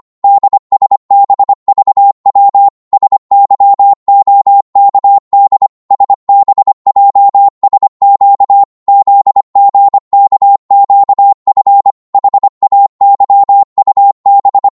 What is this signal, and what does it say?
Morse code, heard as a keyed tone, about 25 wpm, 800 Hz. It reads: DSBVW SYOKD SBJSQ ZGKQF HAYUB
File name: Morse / ltr-25wpm